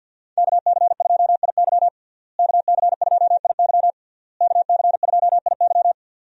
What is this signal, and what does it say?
Morse code, heard as a keyed tone, about 46 wpm, 700 Hz. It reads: KC2IY KC2IY KC2IY
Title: USA Call Signs - 5 Characters - Repeated 3x - 50wpm